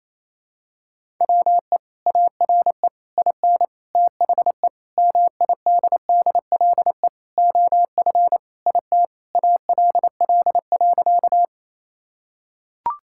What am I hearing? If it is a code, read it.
Morse code, heard as a keyed tone, about 28 wpm, 700 Hz. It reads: WE ARE IN THE MIDDLE OF IT ALL.
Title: Sentences from Top 1000 Words - Review Part 6 - 28wpm